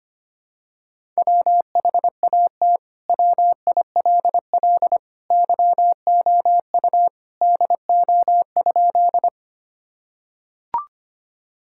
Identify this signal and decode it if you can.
Morse code, heard as a keyed tone, about 25 wpm, 700 Hz. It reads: WHAT WILL YOU DO?
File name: practice / 25wpm / Sentences with 2-Word Phrases - 25wpm